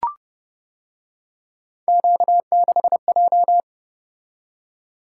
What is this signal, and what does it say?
Morse code, heard as a keyed tone, about 29 wpm, 700 Hz. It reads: Q6J